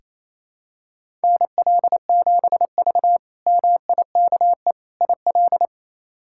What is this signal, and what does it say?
Morse code, heard as a keyed tone, about 28 wpm, 700 Hz. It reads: NL7V MIKE IL
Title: K1USN SST QSO Practice - 28wpm